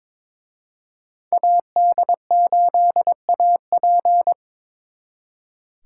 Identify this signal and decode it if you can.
Morse code, heard as a keyed tone, about 26 wpm, 700 Hz. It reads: AD8AP
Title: Speed Racing - USA 2x2 Call Signs 22wpm